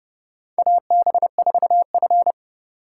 Morse code, heard as a keyed tone, about 30 wpm, 700 Hz. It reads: AB4F
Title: USA Call Signs - 4 Characters - 2x Character Spacing - 30wpm